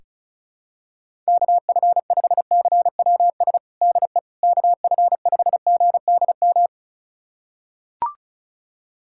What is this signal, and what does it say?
Morse code, heard as a keyed tone, about 34 wpm, 700 Hz. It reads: KF5CWS DE KF5GDM
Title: USA Callsign DE Callsign - 6 Characters - 35wpm